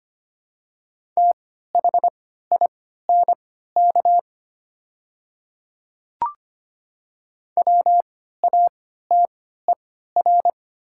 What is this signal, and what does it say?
Morse code, heard as a keyed tone, about 25 wpm, spaced as for 14 wpm, 700 Hz. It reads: THINK WATER
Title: 3x Character Spacing - Top 200 Words - 25wpm